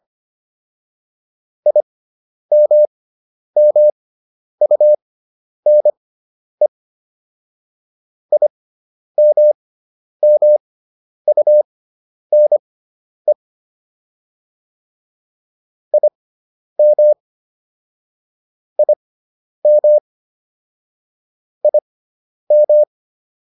Morse code, heard as a keyed tone, about 25 wpm, spaced as for 10 wpm, 600 Hz. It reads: IMMUNE IMMUNE IM IM IM